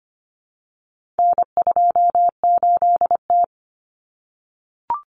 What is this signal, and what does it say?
Morse code, heard as a keyed tone, about 25 wpm, 700 Hz. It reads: N28T
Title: Common Cut Numbers - 4-Digit 25wpm